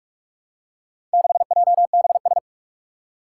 Morse code, heard as a keyed tone, about 44 wpm, 700 Hz. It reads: 6JBS
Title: Four Challenging Letter-Number Combinations 45wpm